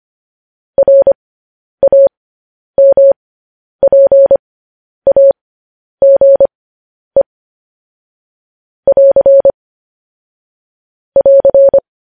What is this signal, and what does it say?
Morse code, heard as a keyed tone, about 25 wpm, spaced as for 10 wpm, 550 Hz. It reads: RAMPAGE <AR> <AR>